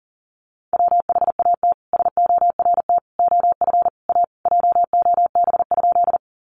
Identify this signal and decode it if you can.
Morse code, heard as a keyed tone, about 40 wpm, 700 Hz. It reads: WHAT SORT OF A JOB?
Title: Sentences from Top 1300 Words 40wpm